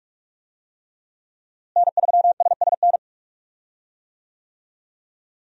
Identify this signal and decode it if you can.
Morse code, heard as a keyed tone, about 44 wpm, 700 Hz. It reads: N3SSN
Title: USA Call Signs - 5 Characters - Repeated 3x - 45wpm